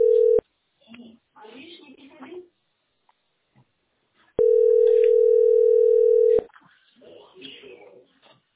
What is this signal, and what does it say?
Ringback tone (North America), 440 + 480 Hz, with speech.
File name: a524baed-178b-4f99-a11a